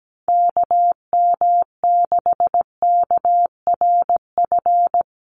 Morse code, heard as a keyed tone, about 17 wpm, 700 Hz. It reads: KM6KRF
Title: USA Call Signs Repeated 3x 17wpm